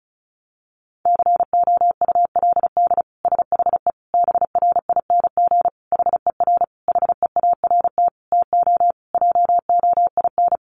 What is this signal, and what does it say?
Morse code, heard as a keyed tone, about 34 wpm, 700 Hz. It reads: COULD SHE BRING HER HEART TO JOIN
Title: Sentences from Top 1400 Words 35wpm